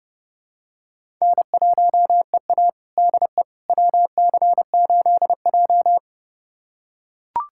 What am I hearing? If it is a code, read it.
Morse code, heard as a keyed tone, about 29 wpm, 700 Hz. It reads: N1EA DE WC8J
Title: USA Callsign DE Callsign - 4 Characters - 30wpm